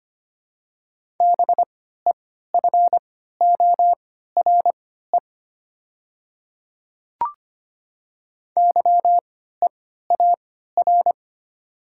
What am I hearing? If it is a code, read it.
Morse code, heard as a keyed tone, about 25 wpm, spaced as for 14 wpm, 690 Hz. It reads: BEFORE YEAR
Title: 3x Character Spacing - Top 200 Words - 25wpm